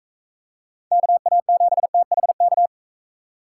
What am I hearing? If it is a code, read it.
Morse code, heard as a keyed tone, about 42 wpm, 700 Hz. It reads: KA7THK